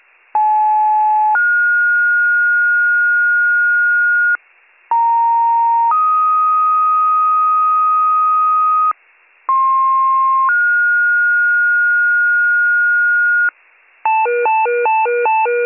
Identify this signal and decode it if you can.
Two-tone page, heard as a one-second tone, then a three-second tone, about 860 then 1430 Hz, about 930 then 1250 Hz, about 1060 then 1430 Hz, with noise. It is heard from 0.3 to 13.5 s.